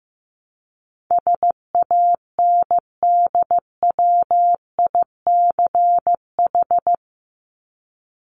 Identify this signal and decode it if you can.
Morse code, heard as a keyed tone, about 15 wpm, 700 Hz. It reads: SANDWICH
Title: Top 1751-2000 Words 15wpm